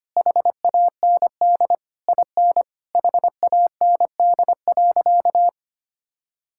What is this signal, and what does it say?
Morse code, heard as a keyed tone, about 25 wpm, 700 Hz. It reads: HAND IN HAND.
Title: Sentences from Top 1000 Words - Review Part 6 - 25wpm